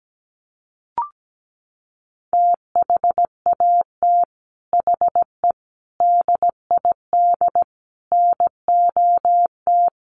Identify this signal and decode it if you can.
Morse code, heard as a keyed tone, about 17 wpm, 700 Hz. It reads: THAT HE DID NOT
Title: Sets of 4 Words 17wpm